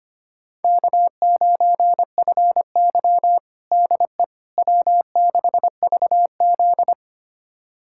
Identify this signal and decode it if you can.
Morse code, heard as a keyed tone, about 25 wpm, 700 Hz. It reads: K9FY DE W6VZ
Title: USA Callsign DE Callsign - 4 Characters - 25wpm